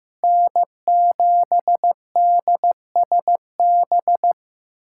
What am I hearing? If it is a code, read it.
Morse code, heard as a keyed tone, about 15 wpm, 700 Hz. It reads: N7DSB
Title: USA Callsign DE Callsign 15wpm